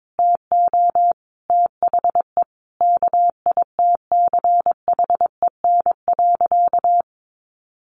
Morse code, heard as a keyed tone, about 22 wpm, 700 Hz. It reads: TO THE KITCHEN.
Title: Sentences from Top 1400 Words 22wpm